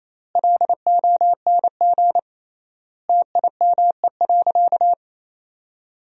Morse code, heard as a keyed tone, about 28 wpm, 700 Hz. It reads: LONG TIME.
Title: 3x Word-Spacing - Sentences from Top 400 Words - 28wpm